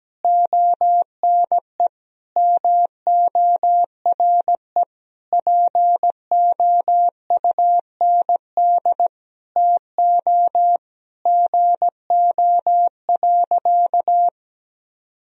Morse code, heard as a keyed tone, about 17 wpm, 700 Hz. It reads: ONE MORE POUND TO GO.
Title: Sentences from Top 1000 Words - Review Part 2 - 17wpm